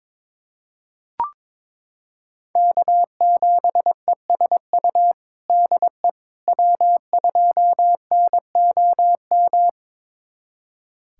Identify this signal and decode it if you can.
Morse code, heard as a keyed tone, about 22 wpm, 700 Hz. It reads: K7ESU DE W2NOM
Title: USA Callsign DE Callsign - 5 Characters - 22wpm